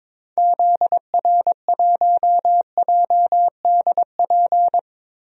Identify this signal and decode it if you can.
Morse code, heard as a keyed tone, about 22 wpm, 700 Hz. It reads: ZR1JDP
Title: DXCC-100 Call Signs - 6 Characters - Repeated 2x - 22wpm